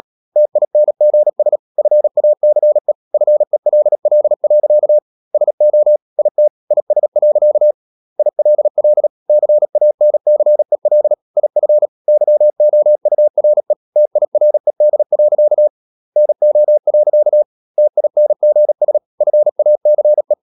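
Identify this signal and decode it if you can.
Morse code, heard as a keyed tone, about 37 wpm, 600 Hz. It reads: TINGS FACE FELL. SO IT IS. ILL CANCEL IF YOURE TIRED. NO. TINGS FACE
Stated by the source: Morse practice at 36 wpm, text of Ken Follett's novel Never, from CW med Gnister.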